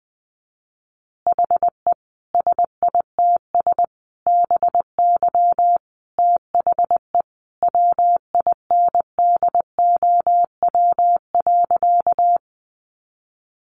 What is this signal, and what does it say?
Morse code, heard as a keyed tone, about 20 wpm, 700 Hz. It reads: HE SITS BY THE WINDOW.
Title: Easy Phrases 20wpm